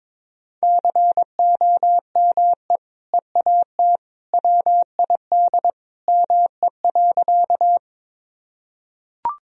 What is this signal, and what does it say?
Morse code, heard as a keyed tone, about 22 wpm, 700 Hz. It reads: COME EAT WID ME.
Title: Sentences from Top 300 Words and Abbreviations 22wpm